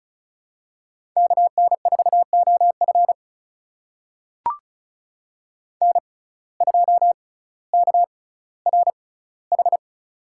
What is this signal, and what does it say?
Morse code, heard as a keyed tone, about 34 wpm, spaced as for 18 wpm, 700 Hz. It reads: KN4OF N2KRH
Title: USA Call Signs - 5 Characters - 6x Character Spacing - 35wpm